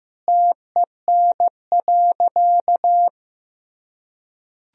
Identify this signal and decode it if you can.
Morse code, heard as a keyed tone, about 15 wpm, 700 Hz. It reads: TEN.
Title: Sentences from Top 1000 Words - Review Part 2 - 15wpm